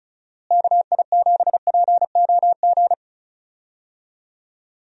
Morse code, heard as a keyed tone, about 34 wpm, 700 Hz. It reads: KI7POG
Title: USA Call Signs - 6 Characters - Repeated 3x - 35wpm